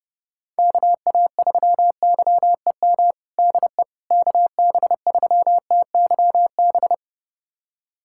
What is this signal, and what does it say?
Morse code, heard as a keyed tone, about 30 wpm, 700 Hz. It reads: KA3YEM DE KB3TYB
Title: USA Callsign DE Callsign 30wpm